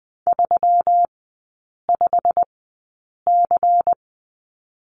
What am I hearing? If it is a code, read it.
Morse code, heard as a keyed tone, about 20 wpm, 700 Hz. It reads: 3 5 C
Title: 121 Lesson 20e - Number 3 - Three Characters 20wpm